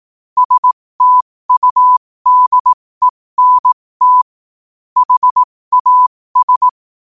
Morse code, heard as a keyed tone, about 18 wpm, spaced as for 15 wpm, 1000 Hz. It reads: STUDENT HAS
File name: qotdfiles / 15wpm